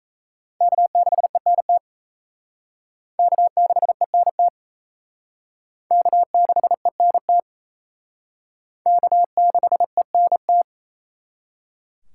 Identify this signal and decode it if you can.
Morse code, heard as a keyed tone, about 34 wpm, 700 Hz. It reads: K6ENT K6ENT K6ENT K6ENT
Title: Speed Racing - USA 1x3 Call Signs 28wpm